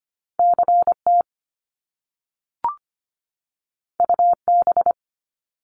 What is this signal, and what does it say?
Morse code, heard as a keyed tone, about 25 wpm, 700 Hz. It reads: CT UB
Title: Top Two Letter Combinations in Words 25wpm